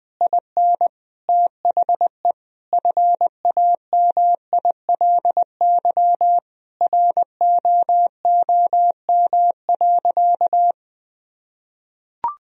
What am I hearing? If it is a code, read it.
Morse code, heard as a keyed tone, about 20 wpm, 700 Hz. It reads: IN THE FAMILY ROOM.
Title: Sentences from Top 1000 Words - Review Part 3 - 20wpm